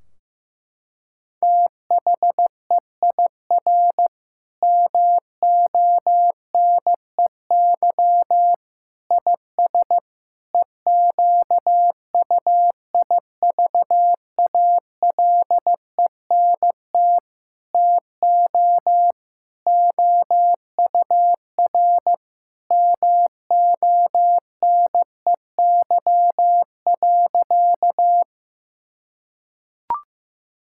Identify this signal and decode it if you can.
Morse code, heard as a keyed tone, about 15 wpm, 700 Hz. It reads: THEIR MONEY IS EQUIVALENT TO OUR MONEY.
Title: Sentences from Top 1600 Words 15wpm